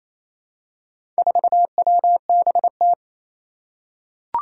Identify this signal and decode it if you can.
Morse code, heard as a keyed tone, about 28 wpm, 700 Hz. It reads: 4WBT
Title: Four Challenging Letter-Number Combinations 28wpm